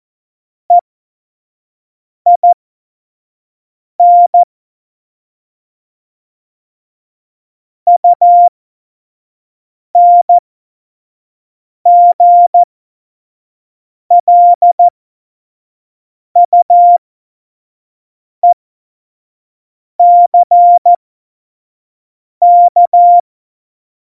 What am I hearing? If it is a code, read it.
Morse code, heard as a keyed tone, about 14 wpm, spaced as for 5 wpm, 700 Hz. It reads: EIN UNGLUECK
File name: Morsekurs / Witze / Witz 29. Mai 2018, 25 F70